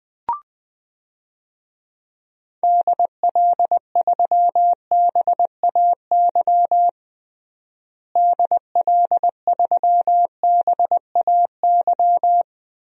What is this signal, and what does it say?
Morse code, heard as a keyed tone, about 21 wpm, 700 Hz. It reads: DL3BAY DL3BAY
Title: DXCC-100 Call Signs - 6 Characters - Repeated 3x - 20wpm